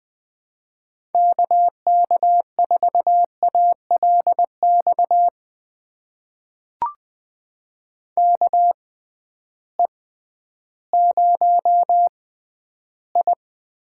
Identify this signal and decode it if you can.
Morse code, heard as a keyed tone, about 20 wpm, spaced as for 12 wpm, 700 Hz. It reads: KK4ALX KE0I
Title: USA Call Signs - 6 Characters - 6x Character Spacing - 20wpm